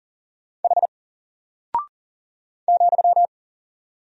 Morse code, heard as a keyed tone, about 40 wpm, 700 Hz. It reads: H ,
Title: Single Character (Letter:Number:Punctuation:Prosign) Alt 40wpm